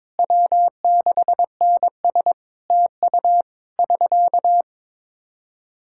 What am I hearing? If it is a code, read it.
Morse code, heard as a keyed tone, about 22 wpm, 700 Hz. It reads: W6NS TU <SK>
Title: Practice QSO - Exchange 4 - Ending - 22wpm